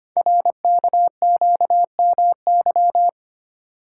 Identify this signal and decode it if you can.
Morse code, heard as a keyed tone, about 25 wpm, 700 Hz. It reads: RKQMY
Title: Code Groups - 5 Letter:Numbers - 25wpm